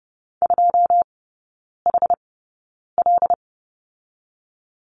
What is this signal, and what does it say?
Morse code, heard as a keyed tone, about 30 wpm, spaced as for 9 wpm, 700 Hz. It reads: 2HL
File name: Lesson 19e - Letter w - Three Characters 30wpm